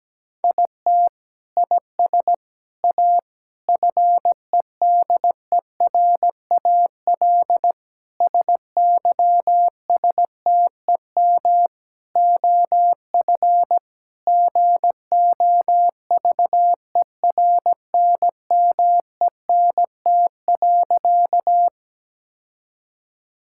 Morse code, heard as a keyed tone, about 17 wpm, 700 Hz. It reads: IT IS A FEDERAL SYSTEM OF GOVERNMENT.
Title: Sentences from Top 1000 Words - Review Part 5 - 17wpm